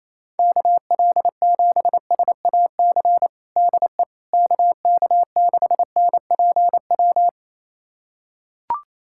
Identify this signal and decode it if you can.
Morse code, heard as a keyed tone, about 27 wpm, 700 Hz. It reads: KL7SAC DE KK6NPW
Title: USA Callsign DE Callsign - 6 Characters - 28wpm